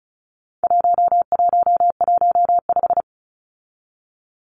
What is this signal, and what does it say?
Morse code, heard as a keyed tone, about 34 wpm, 700 Hz. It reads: 1115
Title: Numbers - 4-Digit 35wpm